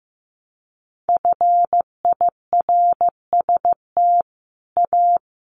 Morse code, heard as a keyed tone, about 15 wpm, 700 Hz. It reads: FIRST A
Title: A Sound Of Thunder 15wpm